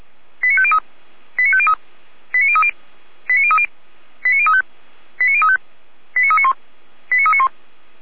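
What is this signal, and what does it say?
Five-tone selective call (ZVEI), heard as quick runs of five tones, tones of about 70 ms. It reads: 89503 89503 89030 89030 89035 89035 89391 89391
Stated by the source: ZVEI five-tone call received by SDR.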